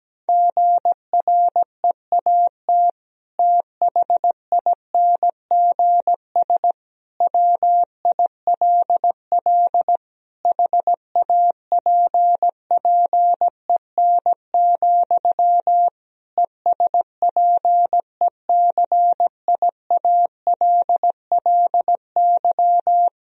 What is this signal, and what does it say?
Morse code, heard as a keyed tone, about 17 wpm, 700 Hz. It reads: GREAT THINGS WILL HAPPEN, ESPECIALLY
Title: Sentences from Top 1000 Words - Review Part 4 - 17wpm